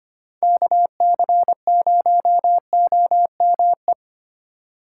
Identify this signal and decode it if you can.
Morse code, heard as a keyed tone, about 25 wpm, 700 Hz. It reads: KC0OME
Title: USA Call Signs - 6 Characters - 2x Character Spacing - 25wpm